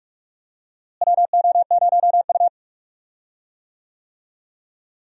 Morse code, heard as a keyed tone, about 46 wpm, 700 Hz. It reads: WO0U
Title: USA Call Signs - 4 Characters - Repeated 3x - 45wpm